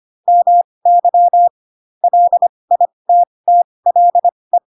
Morse code, heard as a keyed tone, about 25 wpm, spaced as for 20 wpm, 700 Hz. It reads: MY LITTLE
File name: Pinocchio / Pinocchio ch1 15 25 ik1hge